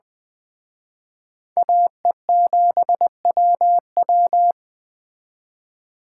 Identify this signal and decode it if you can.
Morse code, heard as a keyed tone, about 21 wpm, 700 Hz. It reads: AE7WW